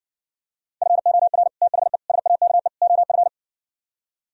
Morse code, heard as a keyed tone, about 69 wpm, 700 Hz. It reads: FOR THE SAKE OF